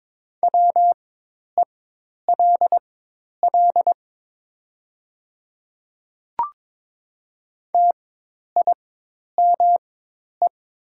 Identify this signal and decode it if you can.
Morse code, heard as a keyed tone, about 22 wpm, spaced as for 10 wpm, 700 Hz. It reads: WELL TIME
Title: 4x Character Spacing - Top 200 Words - 22wpm